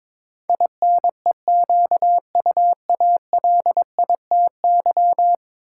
Morse code, heard as a keyed tone, about 22 wpm, 700 Hz. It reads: INEQUALITY
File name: practice / 22wpm / 10 Letter Words - First 3 Letters Given - 22wpm